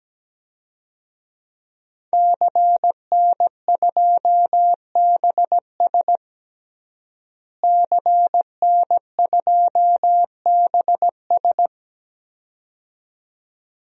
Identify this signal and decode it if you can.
Morse code, heard as a keyed tone, about 17 wpm, 700 Hz. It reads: CN2BS CN2BS
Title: DXCC-100 Call Signs - 5 Characters - Repeated 2x - 17wpm